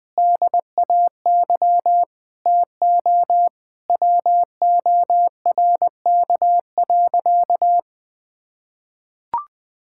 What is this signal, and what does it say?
Morse code, heard as a keyed tone, about 20 wpm, 700 Hz. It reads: DAY TO WORK.
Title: Sentences with 9-Word Phrases - 20wpm